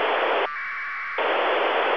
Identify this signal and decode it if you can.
Packet radio (AX.25, as in APRS).